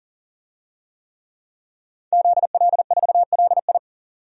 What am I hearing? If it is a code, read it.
Morse code, heard as a keyed tone, about 40 wpm, 700 Hz. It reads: ZL4LI
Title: DXCC-100 Call Signs - 5 Characters - 40wpm